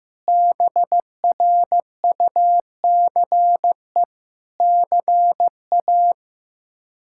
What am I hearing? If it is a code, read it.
Morse code, heard as a keyed tone, about 16 wpm, 700 Hz. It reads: BRUCE CA
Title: K1USN SST QSO Practice - 15wpm